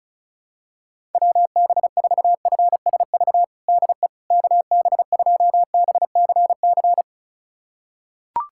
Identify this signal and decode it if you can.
Morse code, heard as a keyed tone, about 36 wpm, 700 Hz. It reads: WB4FSV DE KB2BCC
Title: USA Callsign DE Callsign - 6 Characters - 35wpm